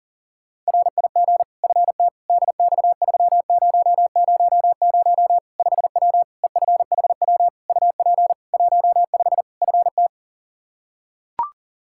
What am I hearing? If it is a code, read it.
Morse code, heard as a keyed tone, about 40 wpm, 700 Hz. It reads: RIG FT DX3000 5W EFHW UP 15 FT
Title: K1USN SST QSO Practice 40wpm